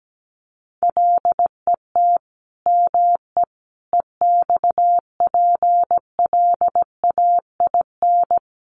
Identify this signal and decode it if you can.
Morse code, heard as a keyed tone, about 17 wpm, 700 Hz. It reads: LET ME EXPLAIN
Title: Sentences from Top 1300 Words 17wpm